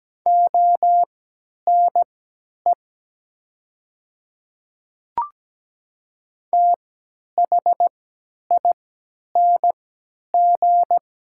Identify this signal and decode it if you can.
Morse code, heard as a keyed tone, about 17 wpm, spaced as for 10 wpm, 700 Hz. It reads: ONE THING
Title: 3x Character Spacing - Top 100 Words - 17wpm